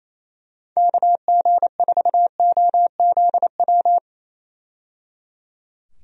Morse code, heard as a keyed tone, about 28 wpm, 700 Hz. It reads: KG4OZW
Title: USA Call Signs - 6 Characters - 28wpm